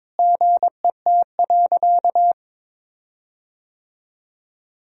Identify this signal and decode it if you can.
Morse code, heard as a keyed tone, about 22 wpm, 700 Hz. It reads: GET.
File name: practice / 22wpm / Sentences with 8-Word Phrases - 22wpm